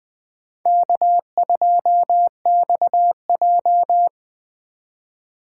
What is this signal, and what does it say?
Morse code, heard as a keyed tone, about 21 wpm, 700 Hz. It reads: K2XJ
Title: Speed Racing - USA 1x2 Call Signs 17wpm